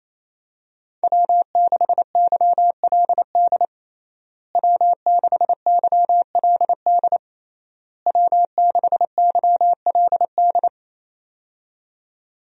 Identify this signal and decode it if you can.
Morse code, heard as a keyed tone, about 28 wpm, 700 Hz. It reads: W6YLD W6YLD W6YLD
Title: USA Call Signs - 5 Characters - Repeated 3x - 28wpm